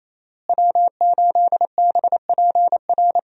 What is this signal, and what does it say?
Morse code, heard as a keyed tone, about 28 wpm, 700 Hz. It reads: W8BPR
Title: USA Call Signs - 5 Characters - Repeated 3x - 28wpm